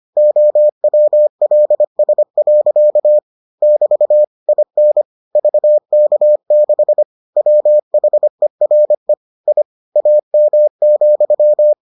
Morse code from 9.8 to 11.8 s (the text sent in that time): AM,